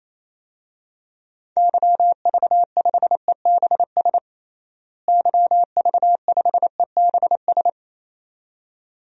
Morse code, heard as a keyed tone, about 27 wpm, 700 Hz. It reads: YV5EBS YV5EBS
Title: DXCC-100 Call Signs - 6 Characters - Repeated 2x - 28wpm